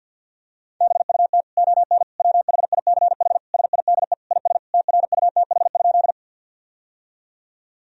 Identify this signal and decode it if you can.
Morse code, heard as a keyed tone, about 49 wpm, 700 Hz. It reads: BUT ON WHICH SIDE IS TRUTH?
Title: Sentences from Top 1000 Words - Review Part 1 - 50wpm